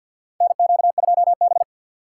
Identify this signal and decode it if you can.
Morse code, heard as a keyed tone, about 51 wpm, 700 Hz. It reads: NQ2B
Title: USA Call Signs - 4 Characters - 3x Character Spacing - 50wpm